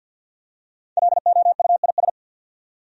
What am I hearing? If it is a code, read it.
Morse code, heard as a keyed tone, about 53 wpm, 700 Hz. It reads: LOUIS
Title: USA Common Names - 50wpm